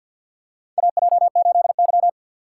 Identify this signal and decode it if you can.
Morse code, heard as a keyed tone, about 50 wpm, 700 Hz. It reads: AJ8Y
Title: USA Call Signs - 2x Character Spacing - 50wpm